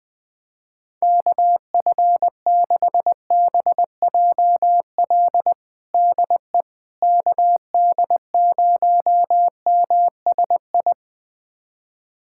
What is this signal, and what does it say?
Morse code, heard as a keyed tone, about 20 wpm, 700 Hz. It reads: KF6BJL DE KD0MSI
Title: USA Callsign DE Callsign - 6 Characters - 20wpm